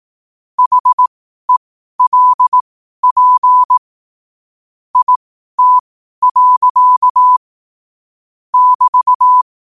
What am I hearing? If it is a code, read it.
Morse code, heard as a keyed tone, about 18 wpm, spaced as for 12 wpm, 1000 Hz. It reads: HELP IT. <BT>
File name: qotdfiles / 12wpm